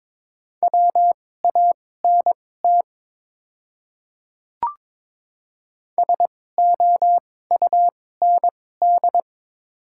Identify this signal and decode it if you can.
Morse code, heard as a keyed tone, about 22 wpm, spaced as for 16 wpm, 700 Hz. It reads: WANT SOUND
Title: 2x Character Spacing - Top 300 Words - 22wpm